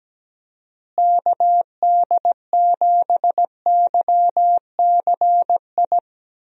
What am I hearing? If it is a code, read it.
Morse code, heard as a keyed tone, about 17 wpm, 700 Hz. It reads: KD7YCI
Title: USA Call Signs - 6 Characters - Repeated 2x - 17wpm